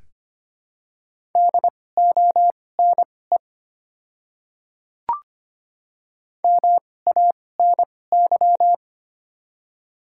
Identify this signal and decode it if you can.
Morse code, heard as a keyed tone, about 25 wpm, spaced as for 18 wpm, 700 Hz. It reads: DONE MANY
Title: 2x Character Spacing - Top 500 Words - 25wpm